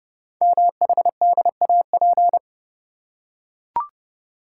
Morse code, heard as a keyed tone, about 28 wpm, 700 Hz. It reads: MHDAP
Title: Code Groups - 5 Letters - 30wpm